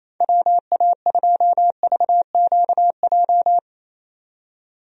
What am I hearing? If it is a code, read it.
Morse code, heard as a keyed tone, about 27 wpm, 700 Hz. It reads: WA2VQJ
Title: USA Call Signs - 6 Characters - 28wpm